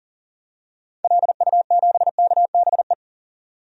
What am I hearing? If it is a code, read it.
Morse code, heard as a keyed tone, about 40 wpm, 700 Hz. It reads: LU7KBE
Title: DXCC-100 Call Signs - 6 Characters - 40wpm